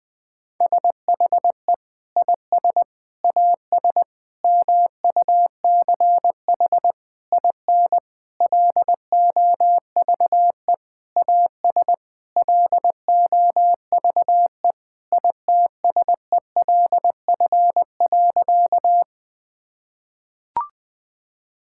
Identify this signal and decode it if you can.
Morse code, heard as a keyed tone, about 20 wpm, 700 Hz. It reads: SHE IS AS MUCH IN LOVE AS LOVE ITSELF.
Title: Sentences from Top 1000 Words - Review Part 5 - 20wpm